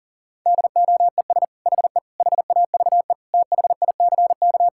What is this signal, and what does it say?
Morse code, heard as a keyed tone, about 37 wpm, 700 Hz. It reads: DOES HE HAVE THICK
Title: Sentences from Top 1600 Words 40wpm